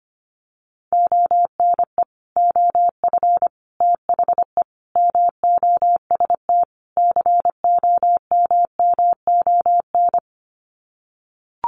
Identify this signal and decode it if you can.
Morse code, heard as a keyed tone, about 25 wpm, 700 Hz. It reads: ONE OF THE MOST COMMON
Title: Sets of 5 Words Encore2 25wpm